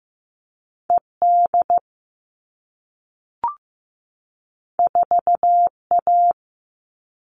Morse code, heard as a keyed tone, about 15 wpm, 700 Hz. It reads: ED 4A
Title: All Cut Numbers - 2-Digit 15wpm